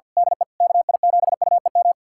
Morse code, heard as a keyed tone, about 46 wpm, 700 Hz. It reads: DE KI7UEM
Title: USA Callsign DE Callsign - 6 Characters - 50wpm